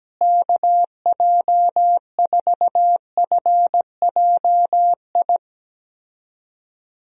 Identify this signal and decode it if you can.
Morse code, heard as a keyed tone, about 17 wpm, 700 Hz. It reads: KJ4FJI
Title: USA Call Signs - 6 Characters - Repeated 2x - 17wpm